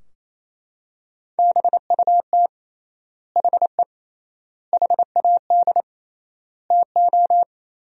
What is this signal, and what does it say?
Morse code, heard as a keyed tone, about 28 wpm, 700 Hz. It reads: BUT HE HAD TO
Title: 3x Word-Spacing - Sentences from Top 100 Words - 28wpm